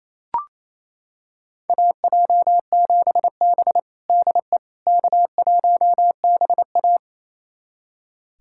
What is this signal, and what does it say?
Morse code, heard as a keyed tone, about 28 wpm, 700 Hz. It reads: AJ7B DE K1BA
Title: USA Callsign DE Callsign - 4 Characters - 28wpm